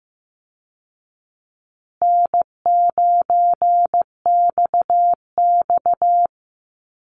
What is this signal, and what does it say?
Morse code, heard as a keyed tone, about 15 wpm, 700 Hz. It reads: N9XX
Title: Easy Call Signs Repeated 2x - 15wpm